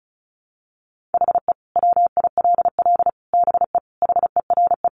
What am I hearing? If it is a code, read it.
Morse code, heard as a keyed tone, about 34 wpm, 700 Hz. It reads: HE WILL BE HERE
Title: Sentences from Top 600 Words 35wpm